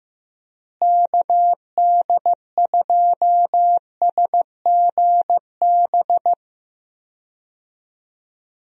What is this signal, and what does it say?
Morse code, heard as a keyed tone, about 15 wpm, 700 Hz. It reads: KD2SGB
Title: USA Call Signs - 6 Characters - Repeated 2x - 15wpm